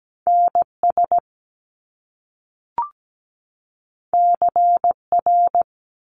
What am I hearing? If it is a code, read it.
Morse code, heard as a keyed tone, about 17 wpm, 700 Hz. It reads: NS CR
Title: Top Two Letter Combinations in Words 17wpm